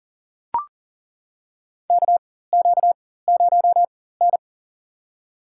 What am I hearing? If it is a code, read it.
Morse code, heard as a keyed tone, about 40 wpm, spaced as for 19 wpm, 700 Hz. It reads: KQ0N
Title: USA Call Signs - 4x Character Spacing - 40wpm